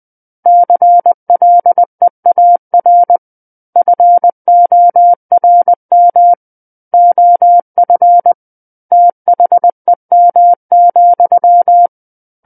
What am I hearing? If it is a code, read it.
Morse code, heard as a keyed tone, about 20 wpm, 700 Hz. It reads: CLEAR FORM OF THEM,
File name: 20 WPM